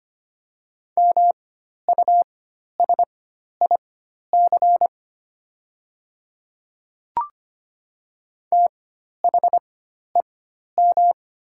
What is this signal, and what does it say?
Morse code, heard as a keyed tone, about 25 wpm, spaced as for 12 wpm, 700 Hz. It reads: MUSIC THEM
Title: 4x Character Spacing - Top 300 Words - 25wpm